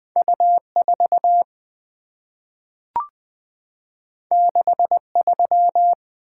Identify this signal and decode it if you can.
Morse code, heard as a keyed tone, about 20 wpm, 700 Hz. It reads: U4 63
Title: Two Letter-Number Combinations - 20wpm